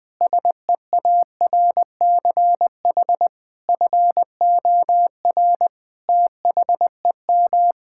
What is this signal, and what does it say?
Morse code, heard as a keyed tone, about 20 wpm, 700 Hz. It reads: SEARCH FOR THEM
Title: Sentences from Top 1000 Words - Review Part 5 - 20wpm